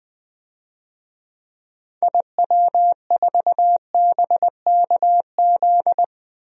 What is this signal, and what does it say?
Morse code, heard as a keyed tone, about 21 wpm, 700 Hz. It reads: IW4BKZ
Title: DXCC-100 Call Signs - 6 Characters - 20wpm